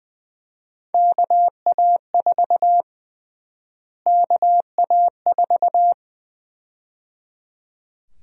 Morse code, heard as a keyed tone, about 20 wpm, 700 Hz. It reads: KA4 KA4
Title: USA Call Sign Prefixes - Repeated 2x - 20wpm